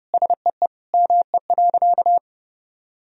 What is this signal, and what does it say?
Morse code, heard as a keyed tone, about 30 wpm, 700 Hz. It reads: SEE ME.
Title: Sentences from Top 1000 Words - Review Part 1 - 30wpm